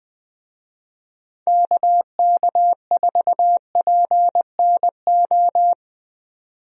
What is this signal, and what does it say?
Morse code, heard as a keyed tone, about 20 wpm, 700 Hz. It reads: KK4PNO
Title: USA Call Signs - 6 Characters - 20wpm